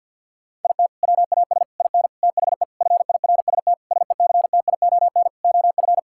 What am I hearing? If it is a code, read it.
Morse code, heard as a keyed tone, about 52 wpm, 700 Hz. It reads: IT WAS IN THE FIRST SECTION OF
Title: Sentences from Top 1000 Words - Review Part 5 - 50wpm